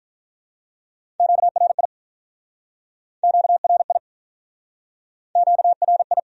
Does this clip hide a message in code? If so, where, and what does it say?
Yes: Morse code from 1.2 to 6.2 s, heard as a keyed tone, about 49 wpm, 700 Hz. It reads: QRI QRI QRI